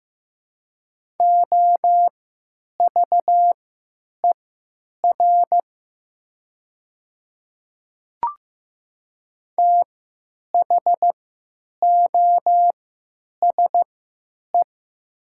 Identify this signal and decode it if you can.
Morse code, heard as a keyed tone, about 15 wpm, spaced as for 9 wpm, 700 Hz. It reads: OVER THOSE
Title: 3x Character Spacing - Top 300 Words - 15wpm